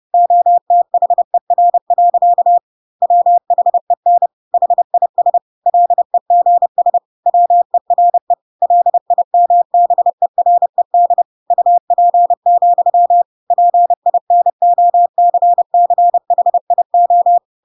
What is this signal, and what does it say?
Morse code, heard as a keyed tone, about 30 wpm, 700 Hz. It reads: OTHER. WHEN HIS LEGS WERE LIMBERED UP, PINOCCHIO
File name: Pinocchio / Pinocchio ch3 30 30 ik1hge